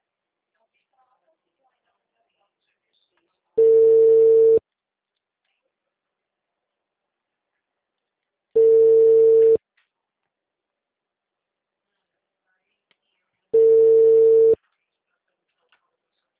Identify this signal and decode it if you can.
Ringback tone (China, 450 Hz), with speech.
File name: out- -2011- - - .75009